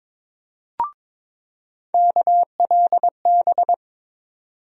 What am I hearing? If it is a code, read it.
Morse code, heard as a keyed tone, about 21 wpm, 700 Hz. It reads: KLB
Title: USA Call Sign Postfixes - 3 Letters 22wpm